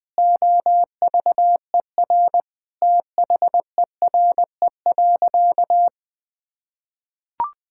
Morse code, heard as a keyed tone, about 21 wpm, 700 Hz. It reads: OVER THERE.
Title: Sentences with 3-Word Phrases - 20wpm